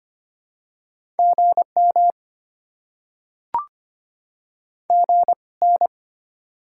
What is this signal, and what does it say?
Morse code, heard as a keyed tone, about 25 wpm, spaced as for 21 wpm, 700 Hz. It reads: GM GN
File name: Top 90 QSO Elements - 2x Character Spacing - 25wpm